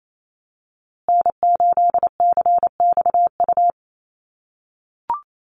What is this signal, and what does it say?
Morse code, heard as a keyed tone, about 28 wpm, 700 Hz. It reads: N8CXU
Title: USA Call Signs Repeated 2x 28wpm